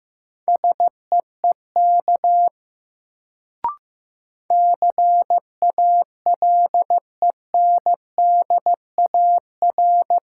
Morse code, heard as a keyed tone, about 15 wpm, 700 Hz. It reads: SEEK CALENDAR
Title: Sentences from Top 1600 Words - New Word Drill Alt - 15wpm